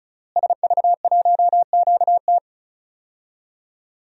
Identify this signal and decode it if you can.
Morse code, heard as a keyed tone, about 36 wpm, 700 Hz. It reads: SV1QT
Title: DXCC-100 Call Signs - 5 Characters - Repeated 3x - 35wpm